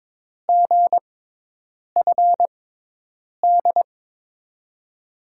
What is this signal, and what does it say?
Morse code, heard as a keyed tone, about 22 wpm, spaced as for 8 wpm, 700 Hz. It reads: GFD
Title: USA Call Signs - 5 Characters - 6x Character Spacing - 22wpm